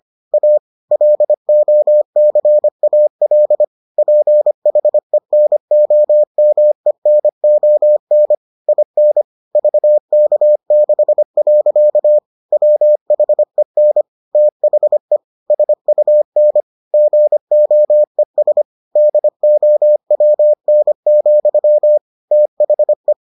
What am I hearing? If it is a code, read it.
Morse code, heard as a keyed tone, about 25 wpm, 600 Hz. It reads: A LOCAL PHENOMENON IN VK6. WHEN THE SUN GOES DOWN, THE